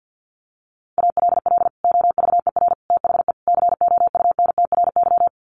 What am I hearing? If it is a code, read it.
Morse code, heard as a keyed tone, about 50 wpm, 700 Hz. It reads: ALL OVER THE COUNTRY
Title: Sets of 4 Words 50wpm